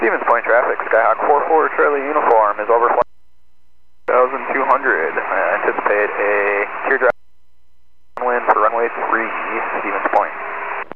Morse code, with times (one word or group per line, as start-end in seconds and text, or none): none